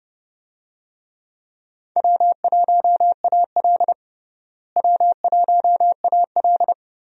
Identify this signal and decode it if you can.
Morse code, heard as a keyed tone, about 30 wpm, 700 Hz. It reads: W1AL W1AL
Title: USA Call Signs - 4 Characters - Repeated 2x - 30wpm